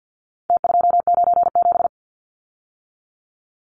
Morse code, heard as a keyed tone, about 50 wpm, 700 Hz. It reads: T297
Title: Numbers - Contest Style - 4-Digit 50wpm